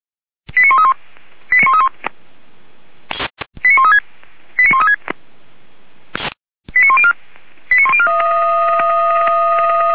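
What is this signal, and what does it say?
Five-tone selective call (ZVEI), heard as quick runs of five tones, tones of about 70 ms. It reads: 89131 89131 89136 89136 89104 89104
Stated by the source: Five-tone ZVEI selective call 89138 received by SDR, 464 channel, from Feuerwehr Orlamünde fire brigade server.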